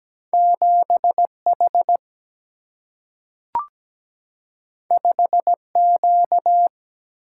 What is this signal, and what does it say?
Morse code, heard as a keyed tone, about 17 wpm, 700 Hz. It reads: 7 H 5 Q
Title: Single-Letter Single-Number 17wpm